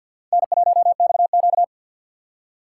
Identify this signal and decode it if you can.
Morse code, heard as a keyed tone, about 51 wpm, 700 Hz. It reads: N1XQ
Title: USA Call Signs - 2x Character Spacing - 50wpm